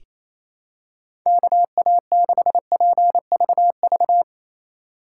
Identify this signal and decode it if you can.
Morse code, heard as a keyed tone, about 27 wpm, 700 Hz. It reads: KA6PVV
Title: USA Call Signs - 6 Characters - 4x Character Spacing - 28wpm